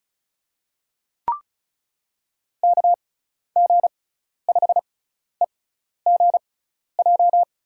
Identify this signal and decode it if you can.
Morse code, heard as a keyed tone, about 36 wpm, spaced as for 12 wpm, 700 Hz. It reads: KG5EGJ
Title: USA Call Signs - 6 Characters - 6x Character Spacing - 35wpm